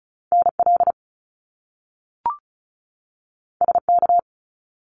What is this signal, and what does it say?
Morse code, heard as a keyed tone, about 36 wpm, 700 Hz. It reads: NL SK
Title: Canadian Provinces and Territories 35wpm